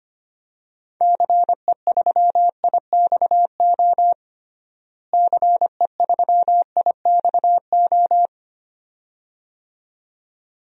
Morse code, heard as a keyed tone, about 25 wpm, 700 Hz. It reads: CE3IXO CE3IXO
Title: DXCC-100 Call Signs - 6 Characters - Repeated 2x - 25wpm